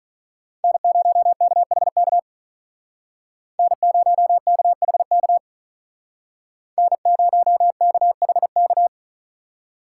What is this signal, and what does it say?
Morse code, heard as a keyed tone, about 44 wpm, 700 Hz. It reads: N0KHK N0KHK N0KHK